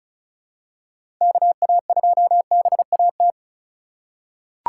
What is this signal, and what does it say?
Morse code, heard as a keyed tone, about 34 wpm, 700 Hz. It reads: KA2BAT
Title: USA Call Signs - 6 Characters - 2x Character Spacing - 35wpm